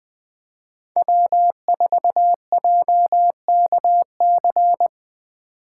Morse code, heard as a keyed tone, about 20 wpm, 700 Hz. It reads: W4JKC
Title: 2023 SOTA Chaser Call Signs - 20wpm